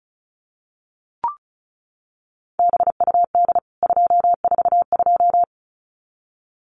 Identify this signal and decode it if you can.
Morse code, heard as a keyed tone, about 34 wpm, 700 Hz. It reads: BUD 242
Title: CWT Exchange 35wpm